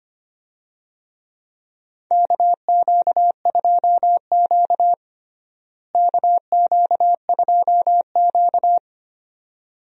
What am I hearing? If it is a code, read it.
Morse code, heard as a keyed tone, about 25 wpm, 700 Hz. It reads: KQ2Q KQ2Q
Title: USA Call Signs - 4 Characters - Repeated 2x - 25wpm